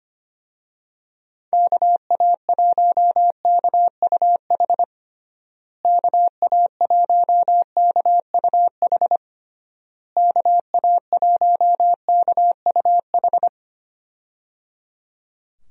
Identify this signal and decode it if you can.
Morse code, heard as a keyed tone, about 25 wpm, 700 Hz. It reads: KA1KUH KA1KUH KA1KUH
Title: USA Call Signs - 6 Characters - Repeated 3x - 25wpm